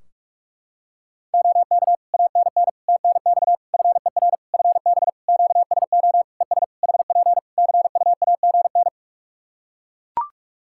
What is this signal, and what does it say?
Morse code, heard as a keyed tone, about 42 wpm, 700 Hz. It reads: OK ANN TNX FER FB QSO ES HP CUAGN
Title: Practice QSO - Exchange 4 - Ending - 45wpm